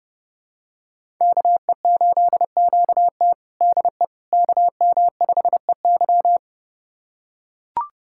Morse code, heard as a keyed tone, about 32 wpm, 700 Hz. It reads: KE8QT DE KM5EY
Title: USA Callsign DE Callsign - 5 Characters - 30wpm